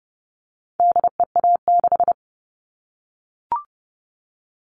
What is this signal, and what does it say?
Morse code, heard as a keyed tone, about 30 wpm, 700 Hz. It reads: DEA6
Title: All Cut Numbers - 4-Digit 30wpm